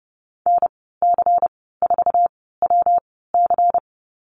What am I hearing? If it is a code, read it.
Morse code, heard as a keyed tone, about 31 wpm, spaced as for 17 wpm, 700 Hz. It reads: NC4WC
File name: Lesson 19g - Letter w - Call Signs 30wpm